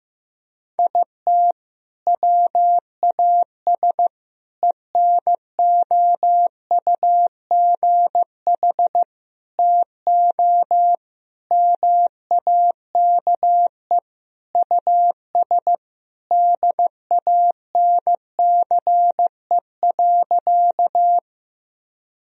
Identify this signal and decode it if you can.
Morse code, heard as a keyed tone, about 15 wpm, 700 Hz. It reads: IT WAS ENOUGH TO MAKE US DANCE.
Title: Sentences from Top 1000 Words - Review Part 4 - 15wpm